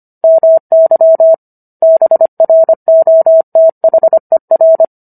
Morse code, heard as a keyed tone, about 25 wpm, 650 Hz. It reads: MY BROTHER